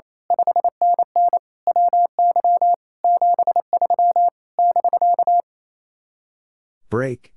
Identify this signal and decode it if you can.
Morse code, heard as a keyed tone, about 27 wpm, 700 Hz. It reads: 5NN WY 73 <BK>
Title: Speed-Racing Common QSO Phrases Encore - 28wpm